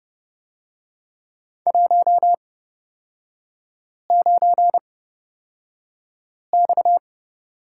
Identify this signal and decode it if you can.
Morse code, heard as a keyed tone, about 30 wpm, spaced as for 5 wpm, 700 Hz. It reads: 19X
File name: Lesson 37e - Letter x - Three Characters 30wpm